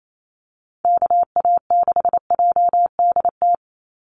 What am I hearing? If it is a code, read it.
Morse code, heard as a keyed tone, about 28 wpm, 700 Hz. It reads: KA6JDT
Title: USA Call Signs 28wpm